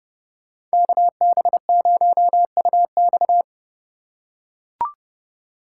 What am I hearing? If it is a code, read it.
Morse code, heard as a keyed tone, about 32 wpm, 700 Hz. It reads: KB0UX
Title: USA Call Signs - 3x Character Spacing - 30wpm